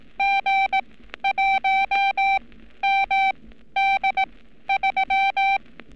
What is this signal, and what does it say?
Morse code, heard as a keyed tone, about 18 wpm, spaced as for 12 wpm, 780 Hz. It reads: G1MD3
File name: training / NAVPERS 11224-11235 / NAVPERS 11227 RA RB